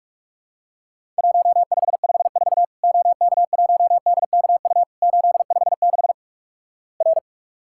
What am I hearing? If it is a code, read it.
Morse code, heard as a keyed tone, about 48 wpm, 700 Hz. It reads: 1554 OK1DKU 856 R
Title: Worked All Europe DX Contest - QTC Exchange - 45wpm